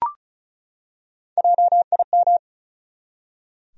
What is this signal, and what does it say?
Morse code, heard as a keyed tone, about 34 wpm, 700 Hz. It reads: JIM
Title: USA Common Names - 35wpm